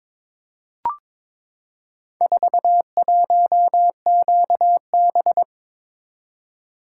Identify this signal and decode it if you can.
Morse code, heard as a keyed tone, about 22 wpm, 700 Hz. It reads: 41QB
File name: practice / 22wpm / Four Letter-Number Combinations - 22wpm